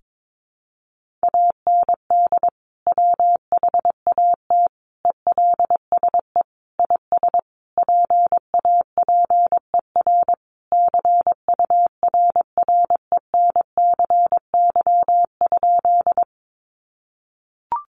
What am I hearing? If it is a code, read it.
Morse code, heard as a keyed tone, about 22 wpm, 700 Hz. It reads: AND WHAT ELSE IS PAPER CURRENCY?
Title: Sentences from Top 1500 Words 22wpm